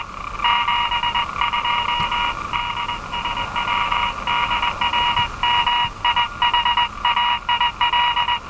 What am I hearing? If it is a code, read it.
Morse code, heard as a keyed tone, about 19 wpm, 970 Hz. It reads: 72 DSW DR MIHAIL
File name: attachments / RV3GM EW1CY